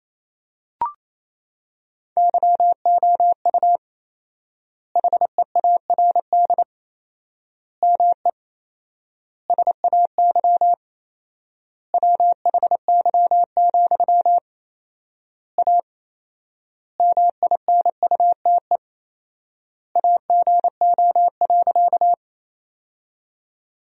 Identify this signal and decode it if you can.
Morse code, heard as a keyed tone, about 27 wpm, 700 Hz. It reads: YOU HEARD ME SAY WHY, A MINUTE AGO.
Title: 4x Word-Spacing - Sentences from Top 500 Words - 28wpm